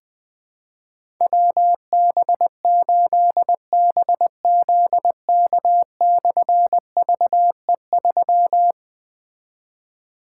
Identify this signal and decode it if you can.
Morse code, heard as a keyed tone, about 20 wpm, 700 Hz. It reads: WB8BZK/VE3
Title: Difficult Call Signs 20wpm